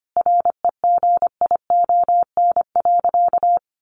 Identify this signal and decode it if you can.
Morse code, heard as a keyed tone, about 25 wpm, 700 Hz. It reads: REGION.
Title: Sentences from Top 1500 Words 25wpm